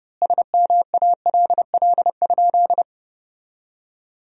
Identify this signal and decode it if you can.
Morse code, heard as a keyed tone, about 32 wpm, 700 Hz. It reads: SMALL?
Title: 4x Word-Spacing - Sentences from Top 200 Words - 30wpm